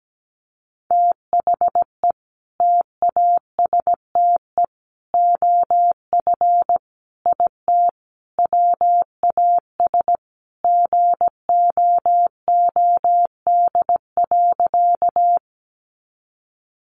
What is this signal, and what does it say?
Morse code, heard as a keyed tone, about 17 wpm, 700 Hz. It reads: THE TASTE OF IT WAS GOOD.
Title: Sentences from Top 1100 Words 17wpm